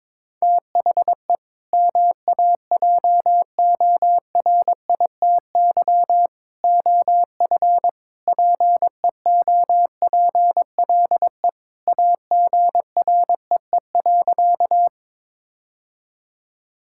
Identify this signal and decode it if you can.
Morse code, heard as a keyed tone, about 22 wpm, 700 Hz. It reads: THE MAJORITY OF PEOPLE AGREE.
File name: practice / 22wpm / Sentences with 3-Word Phrases - 22wpm